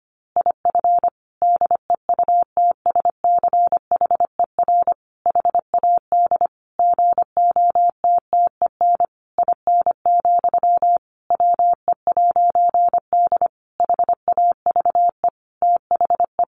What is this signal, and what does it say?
Morse code, heard as a keyed tone, about 25 wpm, 700 Hz. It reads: IF DEUTSCHER HAD GOTTEN IN, WE'D HAVE THE
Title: A Sound Of Thunder 25wpm